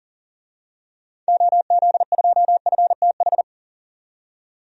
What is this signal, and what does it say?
Morse code, heard as a keyed tone, about 37 wpm, 700 Hz. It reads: OZ2FTH
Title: DXCC-100 Call Signs - 6 Characters - 40wpm